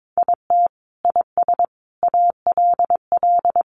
Morse code, heard as a keyed tone, about 22 wpm, 700 Hz. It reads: IT IS ALL
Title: Sentences from Top 500 Words 22wpm